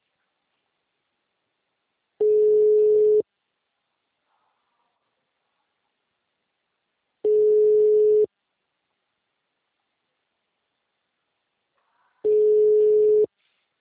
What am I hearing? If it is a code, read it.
Ringback tone (Europe, 425 Hz).
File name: out- -2031- - - .0